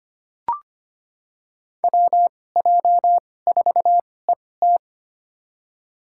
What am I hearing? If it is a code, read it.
Morse code, heard as a keyed tone, about 25 wpm, spaced as for 18 wpm, 700 Hz. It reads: WJ4ET
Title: USA Call Signs - 2x Character Spacing - 25wpm